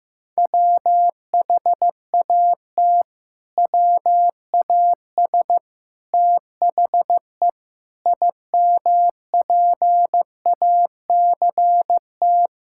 Morse code, heard as a keyed tone, about 16 wpm, 700 Hz. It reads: WHAT WAS THE IMPACT
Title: Sentences from Top 1000 Words - Review Part 6 - 15wpm